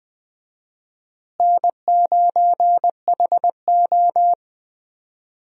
Morse code, heard as a keyed tone, about 20 wpm, 700 Hz. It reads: N9HO
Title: USA Call Signs - 4 Characters - Repeated 2x - 20wpm